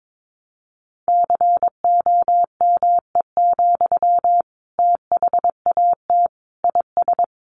Morse code, heard as a keyed tone, about 22 wpm, 700 Hz. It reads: COME, THAT IS
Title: Sentences from Top 700 Words 22wpm